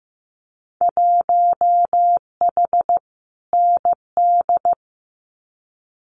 Morse code, heard as a keyed tone, about 15 wpm, 700 Hz. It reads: 1H ND
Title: Winter Field Day Exchanges 15wpm